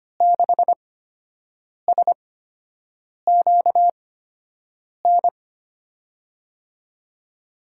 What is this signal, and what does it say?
Morse code, heard as a keyed tone, about 25 wpm, spaced as for 7 wpm, 700 Hz. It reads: 6SQN
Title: USA Call Signs - 6 Characters - 8x Character Spacing - 25wpm